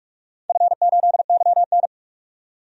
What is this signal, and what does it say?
Morse code, heard as a keyed tone, about 42 wpm, 700 Hz. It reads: F8YN
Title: DXCC-100 Call Signs - 4 Characters - 45wpm